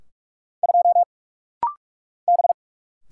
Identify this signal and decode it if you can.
Morse code, heard as a keyed tone, about 44 wpm, 700 Hz. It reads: 2 B
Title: Single Letter-Number - Rapid-Fire 45wpm